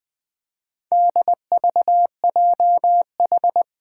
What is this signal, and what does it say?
Morse code, heard as a keyed tone, about 20 wpm, 700 Hz. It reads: DVJH
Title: Four Challenging Letter-Number Combinations 20wpm